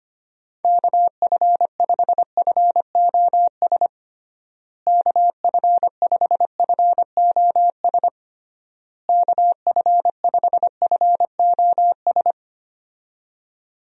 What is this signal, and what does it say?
Morse code, heard as a keyed tone, about 25 wpm, 700 Hz. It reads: KF5FOS KF5FOS KF5FOS
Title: USA Call Signs - 6 Characters - Repeated 3x - 25wpm